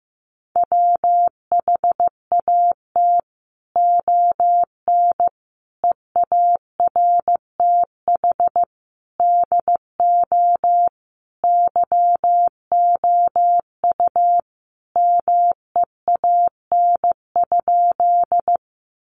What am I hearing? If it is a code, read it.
Morse code, heard as a keyed tone, about 15 wpm, 700 Hz. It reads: WHAT ON EARTH DO YOU MEAN?
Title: Sentences from Top 200 Words 15wpm